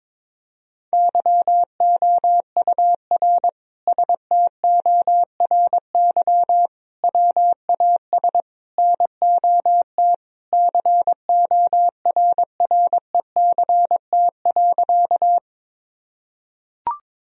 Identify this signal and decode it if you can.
Morse code, heard as a keyed tone, about 22 wpm, 700 Hz. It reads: YOUR STORY WAS NOT CORRECT.
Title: Sentences from Top 1000 Words - Review Part 2 - 22wpm